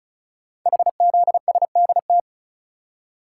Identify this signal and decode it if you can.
Morse code, heard as a keyed tone, about 36 wpm, 690 Hz. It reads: HZSDT
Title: Code Groups - 5 Letters - 35wpm